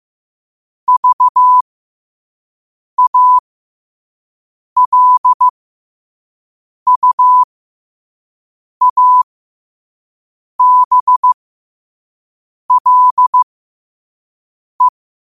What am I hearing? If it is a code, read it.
Morse code, heard as a keyed tone, about 15 wpm, spaced as for 5 wpm, 1000 Hz. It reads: VALUABLE
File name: qotdfiles / 5wpm